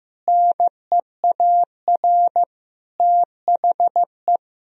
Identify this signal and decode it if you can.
Morse code, heard as a keyed tone, about 15 wpm, 700 Hz. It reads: NEAR THE
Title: Sentences from Top 1000 Words - Review Part 1 - 15wpm